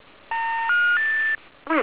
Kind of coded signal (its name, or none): telephone call progress tone